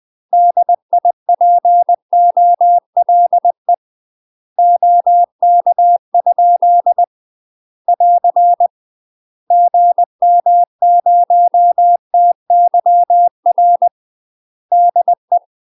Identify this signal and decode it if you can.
Morse code, heard as a keyed tone, about 20 wpm, 700 Hz. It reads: DIPOLE OK? <AR> GM0TYR DE
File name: UK MASTER QSOs TRAINING/20WPM